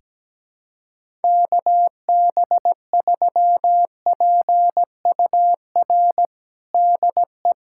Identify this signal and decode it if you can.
Morse code, heard as a keyed tone, about 17 wpm, 700 Hz. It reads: KB3PUR DE
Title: USA Callsign DE Callsign - 6 Characters - 17wpm